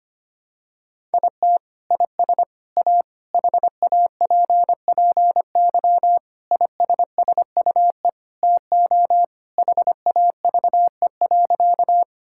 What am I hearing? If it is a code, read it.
Morse code, heard as a keyed tone, about 25 wpm, 700 Hz. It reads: IT IS A HAPPY ISSUE TO HAVE.
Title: Sentences from Top 1000 Words - Review Part 6 - 25wpm